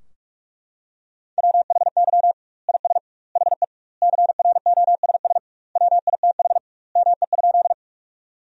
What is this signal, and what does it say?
Morse code, heard as a keyed tone, about 41 wpm, 700 Hz. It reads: WHY IS HE CROSS WITH ME?
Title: 2x Word-Spacing - Sentences from Top 300 Words - 45wpm